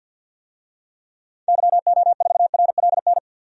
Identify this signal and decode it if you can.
Morse code, heard as a keyed tone, about 52 wpm, 700 Hz. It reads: YO4RLN
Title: DXCC-100 Call Signs - 6 Characters - Repeated 2x - 50wpm